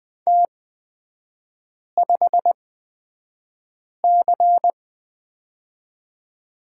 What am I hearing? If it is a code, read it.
Morse code, heard as a keyed tone, about 20 wpm, spaced as for 5 wpm, 690 Hz. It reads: T5C
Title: 229 Lesson 37e - Letter x - Three Characters 20wpm